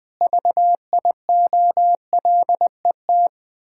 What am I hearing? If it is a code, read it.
Morse code, heard as a keyed tone, about 20 wpm, 700 Hz. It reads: VIOLET
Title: Pangram Challenge v2 - 20wpm